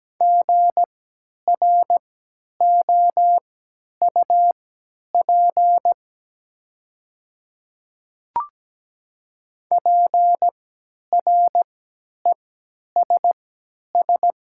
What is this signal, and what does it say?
Morse code, heard as a keyed tone, about 17 wpm, spaced as for 10 wpm, 700 Hz. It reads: GROUP PRESS
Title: 3x Character Spacing - Top 300 Words - 17wpm